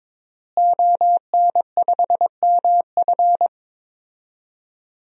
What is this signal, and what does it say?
Morse code, heard as a keyed tone, about 22 wpm, 700 Hz. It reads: ON5MF
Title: DXCC-100 Call Signs - 5 Characters - Repeated 3x - 22wpm